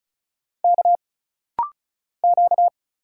Morse code, heard as a keyed tone, about 34 wpm, 700 Hz. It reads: K Q
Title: Single Letter-Number - Rapid-Fire 35wpm